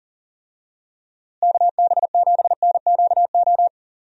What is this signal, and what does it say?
Morse code, heard as a keyed tone, about 40 wpm, 700 Hz. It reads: KB7NQO
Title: USA Call Signs - 6 Characters - Repeated 3x - 40wpm